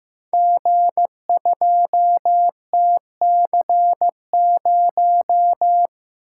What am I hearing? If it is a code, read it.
Morse code, heard as a keyed tone, about 15 wpm, 700 Hz. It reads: G2TC0
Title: Code Groups - 5 Letter:Numbers - 15wpm